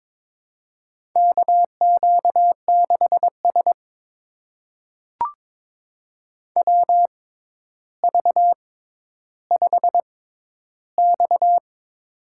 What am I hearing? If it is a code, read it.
Morse code, heard as a keyed tone, about 22 wpm, spaced as for 11 wpm, 700 Hz. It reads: KQ6S WV5X
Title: USA Call Signs - 4 Characters - 6x Character Spacing - 22wpm